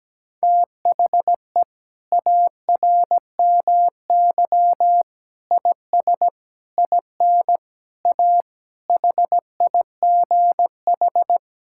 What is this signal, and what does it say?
Morse code, heard as a keyed tone, about 17 wpm, 700 Hz. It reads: THE ARMY IS IN A HIGH
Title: Sentences from Top 1000 Words - Review Part 6 - 17wpm